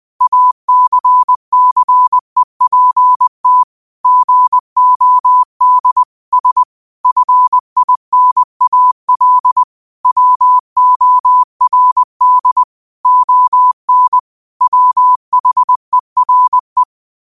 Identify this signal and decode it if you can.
Morse code, heard as a keyed tone, about 20 wpm, 1000 Hz. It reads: ACCEPT GOD S FINAL WORD ON WHERE
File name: qotdfiles / 20wpm